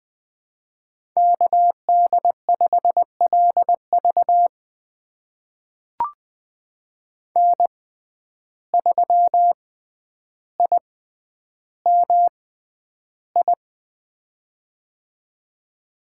Morse code, heard as a keyed tone, about 20 wpm, spaced as for 10 wpm, 700 Hz. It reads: KD5LV N3IMI
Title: USA Call Signs - 5 Characters - 6x Character Spacing - 20wpm